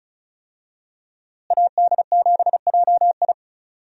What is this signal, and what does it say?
Morse code, heard as a keyed tone, about 34 wpm, 700 Hz. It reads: AD7JI
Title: USA Call Signs - 5 Characters - Repeated 2x - 35wpm